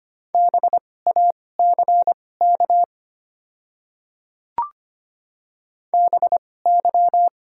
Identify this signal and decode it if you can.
Morse code, heard as a keyed tone, about 25 wpm, spaced as for 18 wpm, 700 Hz. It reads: BACK BY
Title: 2x Character Spacing - Top 300 Words - 25wpm